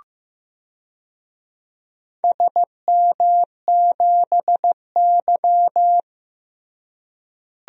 Morse code, heard as a keyed tone, about 15 wpm, 700 Hz. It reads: SM7Y
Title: DXCC-100 Call Signs - 4 Characters - Repeated 3x - 15wpm